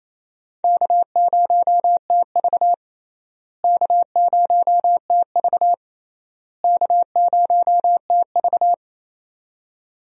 Morse code, heard as a keyed tone, about 28 wpm, 700 Hz. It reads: K0TV K0TV K0TV
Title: USA Call Signs - 4 Characters - Repeated 3x - 28wpm